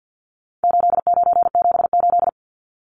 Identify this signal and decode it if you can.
Morse code, heard as a keyed tone, about 50 wpm, 700 Hz. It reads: 8978
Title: Numbers - 4-Digit 50wpm